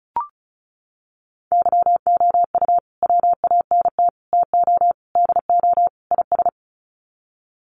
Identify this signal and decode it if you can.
Morse code, heard as a keyed tone, about 34 wpm, 700 Hz. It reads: YOU WANT TO DO IS
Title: Sets of 5 Words Encore2 35wpm